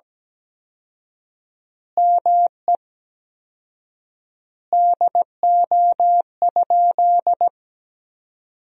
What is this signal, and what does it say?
Morse code, heard as a keyed tone, about 17 wpm, 700 Hz. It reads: ME DO?
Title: 4x Word-Spacing - Sentences from Top 200 Words - 17wpm